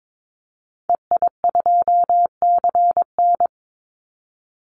Morse code, heard as a keyed tone, about 22 wpm, 700 Hz. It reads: EI2CN
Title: Call Signs - 22wpm